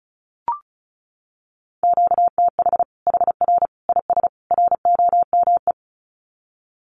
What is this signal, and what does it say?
Morse code, heard as a keyed tone, about 34 wpm, 700 Hz. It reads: QTH HR IS ROME
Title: Common QSO Phrases 35wpm